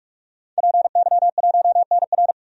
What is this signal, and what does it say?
Morse code, heard as a keyed tone, about 46 wpm, 700 Hz. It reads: PY1NR
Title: DXCC-100 Call Signs - 5 Characters - 45wpm